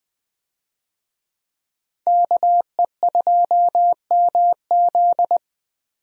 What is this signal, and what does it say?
Morse code, heard as a keyed tone, about 20 wpm, 700 Hz. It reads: KE2MZ
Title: USA Call Signs - 5 Characters - 20wpm